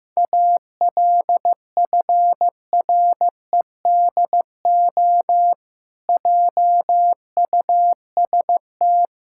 Morse code, heard as a keyed tone, about 16 wpm, 700 Hz. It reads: ALFREDO JUST
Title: Pangram Challenge v2 - 15wpm